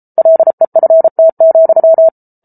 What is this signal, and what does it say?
Morse code, heard as a keyed tone, about 34 wpm, 650 Hz. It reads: LEFT,